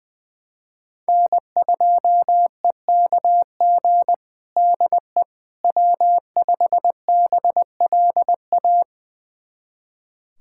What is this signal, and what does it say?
Morse code, heard as a keyed tone, about 20 wpm, 700 Hz. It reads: N2EKG DE W5BLA
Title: USA Callsign DE Callsign - 5 Characters - 20wpm